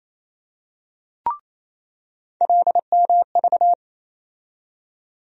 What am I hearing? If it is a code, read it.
Morse code, heard as a keyed tone, about 28 wpm, 700 Hz. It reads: LMV
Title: USA Call Sign Postfixes - 3 Letters 28wpm